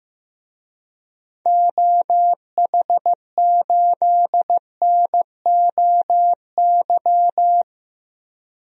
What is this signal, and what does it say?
Morse code, heard as a keyed tone, about 15 wpm, 700 Hz. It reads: OH8NOY
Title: DXCC-100 Call Signs - 6 Characters - Repeated 3x - 15wpm